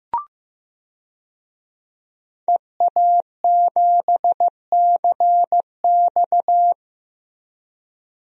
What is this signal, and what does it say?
Morse code, heard as a keyed tone, about 15 wpm, 700 Hz. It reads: EA7CX
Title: DXCC-100 Call Signs - 5 Characters - Repeated 2x - 15wpm